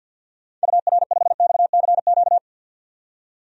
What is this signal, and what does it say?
Morse code, heard as a keyed tone, about 52 wpm, 700 Hz. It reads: UR5XCQ
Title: DXCC-100 Call Signs - 6 Characters - 50wpm